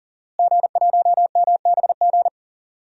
Morse code, heard as a keyed tone, about 40 wpm, 700 Hz. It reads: G1MBG
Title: DXCC-100 Call Signs - 5 Characters - Repeated 3x - 40wpm